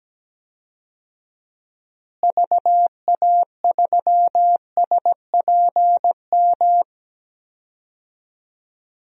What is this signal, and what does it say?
Morse code, heard as a keyed tone, about 17 wpm, 700 Hz. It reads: VA3SPM
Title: DXCC-100 Call Signs - 6 Characters - Repeated 3x - 17wpm